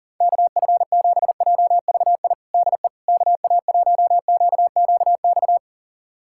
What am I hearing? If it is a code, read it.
Morse code, heard as a keyed tone, about 37 wpm, 700 Hz. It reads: KF7JVI DE KA1QQX
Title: USA Callsign DE Callsign - 6 Characters - 40wpm